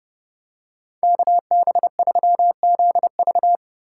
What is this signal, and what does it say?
Morse code, heard as a keyed tone, about 29 wpm, 700 Hz. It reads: KB3ZV